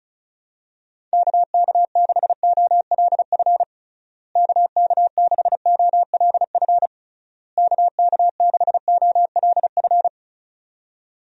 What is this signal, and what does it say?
Morse code, heard as a keyed tone, about 34 wpm, 700 Hz. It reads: KK6OLF KK6OLF KK6OLF
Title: USA Call Signs - 6 Characters - Repeated 3x - 35wpm